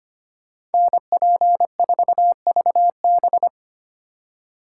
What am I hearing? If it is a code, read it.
Morse code, heard as a keyed tone, about 33 wpm, 700 Hz. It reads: NP4VB